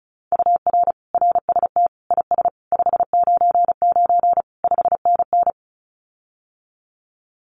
Morse code, heard as a keyed tone, about 35 wpm, 700 Hz. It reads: UR RST IS 599 5NN
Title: Common QSO Phrases Encore 35wpm